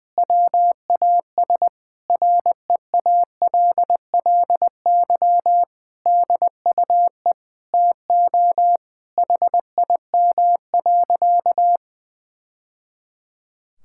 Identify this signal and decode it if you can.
Morse code, heard as a keyed tone, about 20 wpm, 700 Hz. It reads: WAS REALLY DUE TO HIM.
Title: Sentences from Top 1000 Words - Review Part 6 - 20wpm